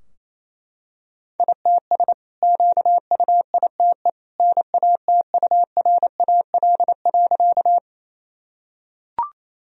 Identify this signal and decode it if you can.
Morse code, heard as a keyed tone, about 27 wpm, 700 Hz. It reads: ITS QUITE NATURAL.
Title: Sentences from Top 1000 Words - Review Part 4 - 28wpm